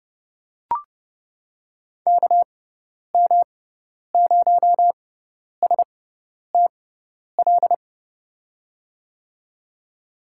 Morse code, heard as a keyed tone, about 30 wpm, spaced as for 10 wpm, 700 Hz. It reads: KM0STL
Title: USA Call Signs - 6x Character Spacing - 30wpm